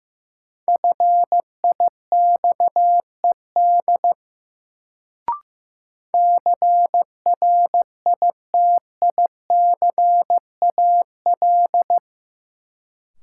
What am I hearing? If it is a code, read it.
Morse code, heard as a keyed tone, about 16 wpm, 700 Hz. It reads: FIXED CRITICAL
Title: Sentences from Top 1100 Words - New Word Drill Alt - 15wpm